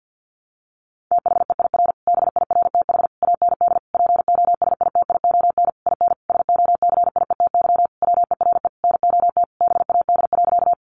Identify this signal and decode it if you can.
Morse code, heard as a keyed tone, about 49 wpm, 700 Hz. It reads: THEIR BIRTH AND POSITION IN SOCIETY WERE NOT BAD.
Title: Sentences from Top 900 Words 50wpm